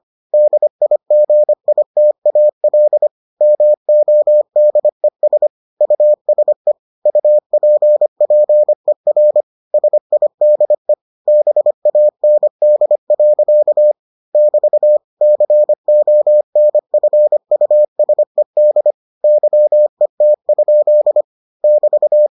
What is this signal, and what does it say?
Morse code, heard as a keyed tone, about 25 wpm, 600 Hz. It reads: DIGITAL MODES USE UPPER SIDE BAND. <BT> CONFUSED YET? <BT>